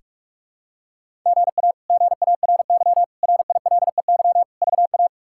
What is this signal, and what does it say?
Morse code, heard as a keyed tone, about 41 wpm, 700 Hz. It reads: GA GARY RILEY VA
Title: K1USN SST QSO Practice - 45wpm